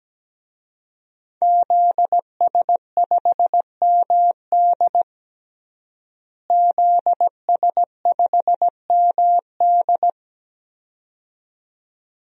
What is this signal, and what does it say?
Morse code, heard as a keyed tone, about 17 wpm, 700 Hz. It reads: ZS5MD ZS5MD
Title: DXCC-100 Call Signs - 5 Characters - Repeated 2x - 17wpm